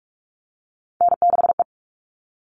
Morse code, heard as a keyed tone, about 44 wpm, 700 Hz. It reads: N6E
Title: All Cut Numbers - 3-Digit 45wpm